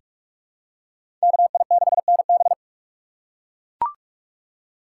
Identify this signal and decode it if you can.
Morse code, heard as a keyed tone, about 42 wpm, 700 Hz. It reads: KI6NB
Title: USA Call Signs - 5 Characters - 4x Character Spacing - 45wpm